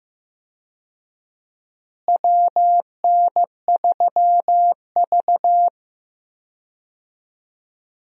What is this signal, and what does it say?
Morse code, heard as a keyed tone, about 15 wpm, 700 Hz. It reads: WN3V
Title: USA Call Signs - 4 Characters - Repeated 3x - 15wpm